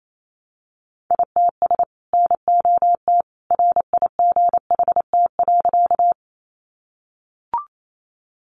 Morse code, heard as a keyed tone, about 28 wpm, 700 Hz. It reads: ITS NOT RIGHT.
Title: Sentences from Top 600 Words 28wpm